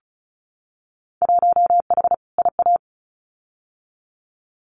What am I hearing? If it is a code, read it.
Morse code, heard as a keyed tone, about 34 wpm, 700 Hz. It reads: 1H IA
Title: Winter Field Day Exchanges 35wpm